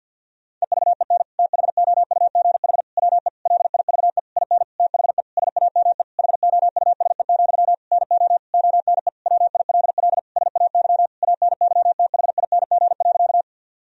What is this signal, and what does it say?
Morse code, heard as a keyed tone, about 50 wpm, 700 Hz. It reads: EVEN THOUGH WE LIVE IN THE SAME HOUSE, NO ONE WILL SAY ANYTHING.
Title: Sentences from Top 1000 Words - Review Part 2 - 50wpm